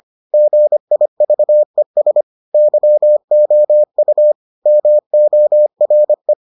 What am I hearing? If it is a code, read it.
Morse code, heard as a keyed tone, about 25 wpm, 600 Hz. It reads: GIVES YOU MORE